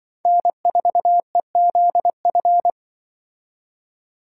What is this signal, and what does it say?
Morse code, heard as a keyed tone, about 24 wpm, 700 Hz. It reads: N4EZF
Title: Speed Racing - USA 1x3 Call Signs 20wpm